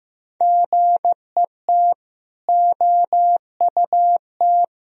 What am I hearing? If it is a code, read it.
Morse code, heard as a keyed tone, about 15 wpm, 700 Hz. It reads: GET OUT
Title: Sentences from Top 200 Words and Abbreviations 15wpm